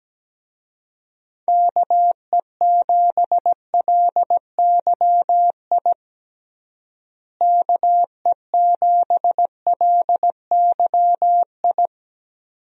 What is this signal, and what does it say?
Morse code, heard as a keyed tone, about 17 wpm, 700 Hz. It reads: KE7LYI KE7LYI
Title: USA Call Signs - 6 Characters - Repeated 2x - 17wpm